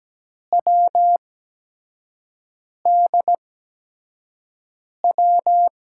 Morse code, heard as a keyed tone, about 17 wpm, spaced as for 5 wpm, 700 Hz. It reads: WDW
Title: USA Call Signs - 8x Character Spacing - 17wpm